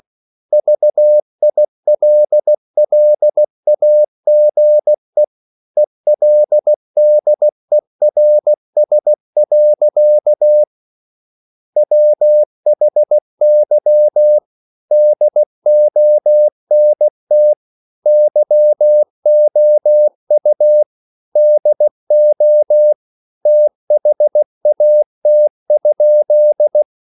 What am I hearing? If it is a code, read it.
Morse code, heard as a keyed tone, about 16 wpm, 600 Hz. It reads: VILLAGE ELDERS. WHY DONT YOU DO THAT?